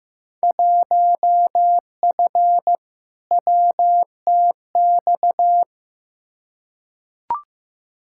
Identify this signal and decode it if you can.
Morse code, heard as a keyed tone, about 15 wpm, 700 Hz. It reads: 1F WTX
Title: Field Day Exchanges Encore - 15wpm